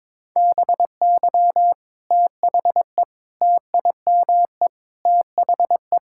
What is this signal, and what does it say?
Morse code, heard as a keyed tone, about 22 wpm, 700 Hz. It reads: BY THE TIME THE
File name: practice / 22wpm / Sentences with 4-Word Phrases - 22wpm